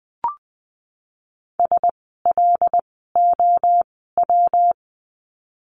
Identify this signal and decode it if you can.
Morse code, heard as a keyed tone, about 20 wpm, spaced as for 15 wpm, 700 Hz. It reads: SLOW
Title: 115 Lesson 19f - Letter w - Words 20wpm